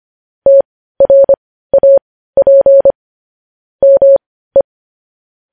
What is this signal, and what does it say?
Morse code, heard as a keyed tone, about 25 wpm, spaced as for 15 wpm, 550 Hz. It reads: TRAP ME
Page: news-headlines-in-morse-code-at-15-wpm